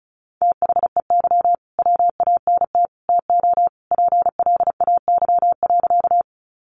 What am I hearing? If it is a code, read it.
Morse code, heard as a keyed tone, about 34 wpm, 700 Hz. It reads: THEY WANT TO PLAY.
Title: Easy Phrases 35wpm